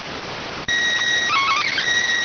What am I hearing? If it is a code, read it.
SSTV (slow-scan television), PD 290.